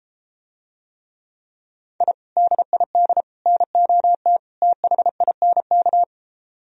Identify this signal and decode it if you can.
Morse code, heard as a keyed tone, about 33 wpm, 700 Hz. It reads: I DID NOT THINK